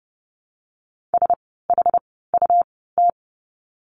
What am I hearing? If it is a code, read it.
Morse code, heard as a keyed tone, about 30 wpm, spaced as for 17 wpm, 700 Hz. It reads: SHUT
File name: Lesson 17f - Letter u - Words 30wpm